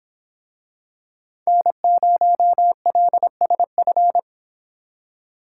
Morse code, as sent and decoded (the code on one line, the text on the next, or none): -. ----- .-.. ... ..-.
N0LSF